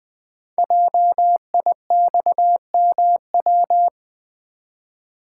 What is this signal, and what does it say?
Morse code, heard as a keyed tone, about 20 wpm, 700 Hz. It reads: JIXMW
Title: Code Groups - 5 Letters - 20wpm